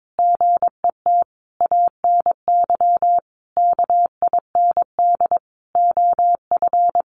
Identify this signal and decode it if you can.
Morse code, heard as a keyed tone, about 22 wpm, 700 Hz. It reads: GET ANY KIND OF
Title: Sentences from Top 1100 Words 22wpm